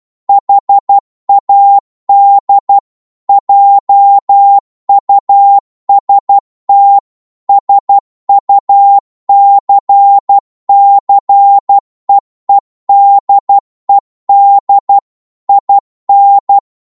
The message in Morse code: HAD JUST SUCCEEDED IN